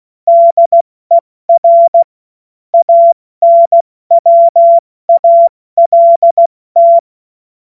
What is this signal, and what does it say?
Morse code, heard as a keyed tone, about 16 wpm, 680 Hz. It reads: DER ANWALT
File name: Morsekurs / Witze / Witz 25. Oktober 2022 70 F80